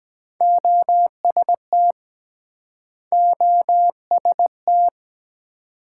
Morse code, heard as a keyed tone, about 23 wpm, 700 Hz. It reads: OST OST